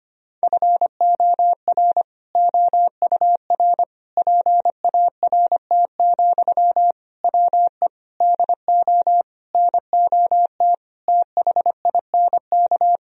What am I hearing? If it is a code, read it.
Morse code, heard as a keyed tone, about 25 wpm, 700 Hz. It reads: FOR OUR PART, WE DO NOT THINK
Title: Sentences from Top 1000 Words - Review Part 1 - 25wpm